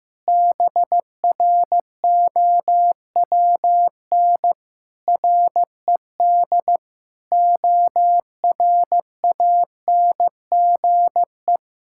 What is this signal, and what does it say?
Morse code, heard as a keyed tone, about 15 wpm, 700 Hz. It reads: BROWN RED ORANGE
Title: Pangram Challenge v2 - 15wpm